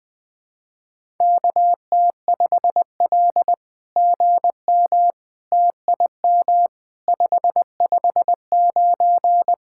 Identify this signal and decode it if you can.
Morse code, heard as a keyed tone, about 20 wpm, 700 Hz. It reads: KT5L GM TIM 559
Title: SOTA QSO Practice 20wpm